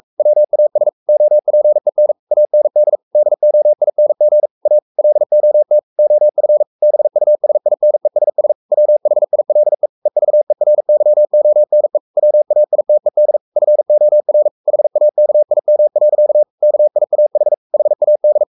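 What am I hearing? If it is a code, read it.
Morse code, heard as a keyed tone, about 44 wpm, 600 Hz. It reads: WAS OPEN AND DOING A LOT OF BUSINESS WHILE EVERYONE WAITED FOR HAKIM. KIAH HAD
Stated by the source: Morse code at 43 wpm sending Ken Follett's novel Never in English, from CW med Gnister.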